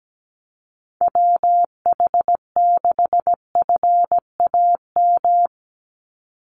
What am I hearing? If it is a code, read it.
Morse code, heard as a keyed tone, about 17 wpm, 700 Hz. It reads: WH6FAM
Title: USA Call Signs 17wpm